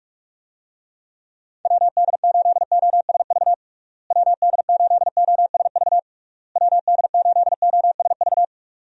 Morse code, heard as a keyed tone, about 41 wpm, 700 Hz. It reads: WD8OSV WD8OSV WD8OSV
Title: USA Call Signs - 6 Characters - Repeated 3x - 45wpm